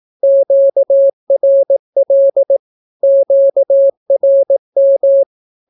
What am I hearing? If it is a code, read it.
Morse code, heard as a keyed tone, about 18 wpm, 550 Hz. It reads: QRL QRM
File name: naukapodstaw / skroty / skroty